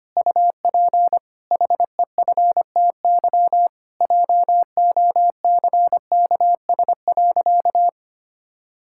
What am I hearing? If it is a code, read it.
Morse code, heard as a keyed tone, about 25 wpm, 700 Hz. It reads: UP HEFTY JOCKS.
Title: Pangram Challenge v2 - 25wpm